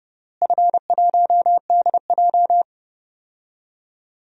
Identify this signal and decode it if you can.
Morse code, heard as a keyed tone, about 28 wpm, 700 Hz. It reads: F1DJ
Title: DXCC-100 Call Signs - 4 Characters - Repeated 3x - 30wpm